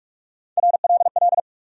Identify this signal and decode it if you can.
Morse code, heard as a keyed tone, about 46 wpm, 700 Hz. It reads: RLL
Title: USA Call Sign Postfixes - 3 Letters 45wpm